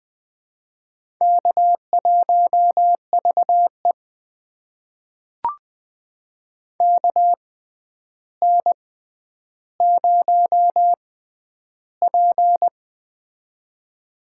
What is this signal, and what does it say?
Morse code, heard as a keyed tone, about 20 wpm, spaced as for 10 wpm, 700 Hz. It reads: K1VE KN0P
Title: USA Call Signs - 6x Character Spacing - 20wpm